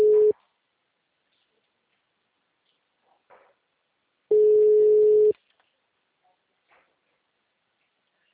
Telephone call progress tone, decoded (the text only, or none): ringback tone